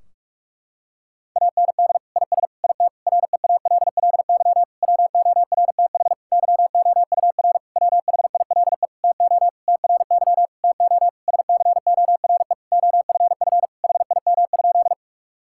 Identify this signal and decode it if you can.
Morse code, heard as a keyed tone, about 46 wpm, 700 Hz. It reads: AND IS IT REALLY WORTH YOUR WHILE TO TRY TO SCORE OFF HIM?
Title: Sentences from Top 1600 Words 45wpm